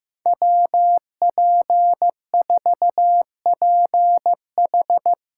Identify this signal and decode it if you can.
Morse code, heard as a keyed tone, about 15 wpm, 700 Hz. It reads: WP4PH
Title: USA Call Signs - 6 Characters - 8x Character Spacing - 15wpm